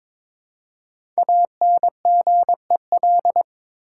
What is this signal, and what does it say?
Morse code, heard as a keyed tone, about 22 wpm, 700 Hz. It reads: ANGEL
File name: 5 Letter Words - First 3 Letters Given - 22wpm